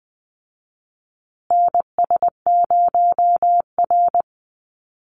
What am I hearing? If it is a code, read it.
Morse code, heard as a keyed tone, about 20 wpm, 700 Hz. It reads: NS0R
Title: Call Signs Repeated 2x - 20wpm